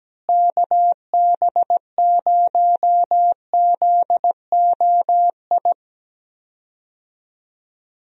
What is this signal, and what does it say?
Morse code, heard as a keyed tone, about 17 wpm, 700 Hz. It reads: KB0ZOI
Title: USA Call Signs - 6 Characters - Repeated 2x - 17wpm